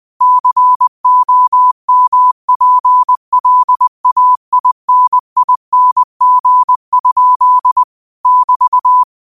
Morse code, heard as a keyed tone, about 20 wpm, 1000 Hz. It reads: COMPLAINING? <BT>
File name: 20wpm